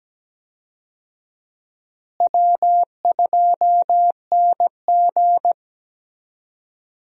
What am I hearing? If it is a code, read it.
Morse code, heard as a keyed tone, about 17 wpm, 700 Hz. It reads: W2NG
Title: USA Call Signs - 4 Characters - Repeated 3x - 17wpm